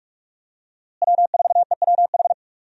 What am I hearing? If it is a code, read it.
Morse code, heard as a keyed tone, about 42 wpm, 700 Hz. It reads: W4EWH
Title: USA Call Signs - 5 Characters - Repeated 3x - 45wpm